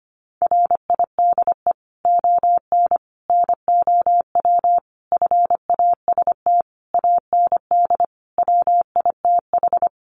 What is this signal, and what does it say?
Morse code, heard as a keyed tone, about 25 wpm, 700 Hz. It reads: RIDE ON NOW FAST AND WITH
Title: Sentences from Top 1500 Words 25wpm